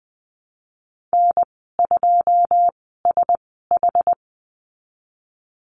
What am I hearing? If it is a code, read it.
Morse code, heard as a keyed tone, about 20 wpm, spaced as for 15 wpm, 700 Hz. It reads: N2SH
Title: 088 Lesson 15g - Number 5 - Call Signs 20wpm